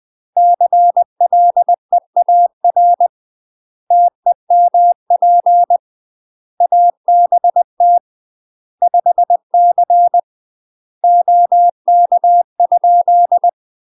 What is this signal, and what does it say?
Morse code, heard as a keyed tone, about 20 wpm, 700 Hz. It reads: CLEAR TEMP ABT 5C OK?
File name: UK MASTER QSOs TRAINING / 20WPM / UK MASTER QSOs 0007